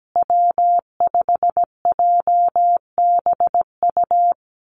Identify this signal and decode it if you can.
Morse code, heard as a keyed tone, about 17 wpm, 700 Hz. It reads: W5JBU
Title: USA Call Signs Repeated 3x 17wpm